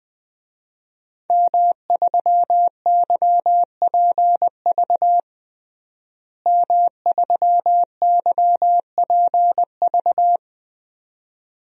Morse code, heard as a keyed tone, about 21 wpm, 700 Hz. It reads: M3YPV M3YPV
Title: DXCC-100 Call Signs - 5 Characters - Repeated 2x - 20wpm